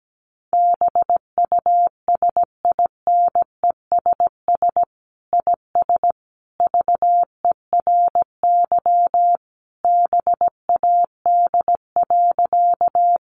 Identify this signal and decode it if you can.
Morse code, heard as a keyed tone, about 17 wpm, 700 Hz. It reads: BUSINESS IS VERY BAD.
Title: Sentences from Top 700 Words 17wpm